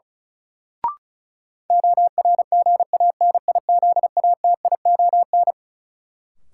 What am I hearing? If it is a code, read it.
Morse code, heard as a keyed tone, about 34 wpm, 700 Hz. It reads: ORGANIZATION
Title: Sentences from Top 1000 Words - Review Part 5 - New Word Drill - 35wpm